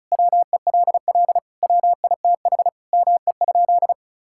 Morse code, heard as a keyed tone, about 34 wpm, 700 Hz. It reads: WELL WITH ME?
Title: Sentences from Top 1600 Words 35wpm